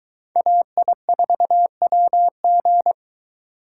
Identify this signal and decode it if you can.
Morse code, heard as a keyed tone, about 23 wpm, 700 Hz. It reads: AI4WG